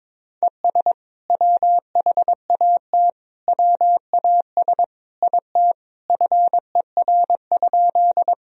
Morse code, heard as a keyed tone, about 22 wpm, 700 Hz. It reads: ES WHAT WAS IT FER?
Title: Sentences from Top 100 Words and Abbreviations 22wpm